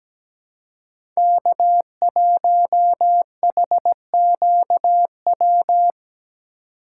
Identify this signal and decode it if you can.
Morse code, heard as a keyed tone, about 17 wpm, 700 Hz. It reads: K1HQW
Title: USA Call Signs - 4x Character Spacing - 17wpm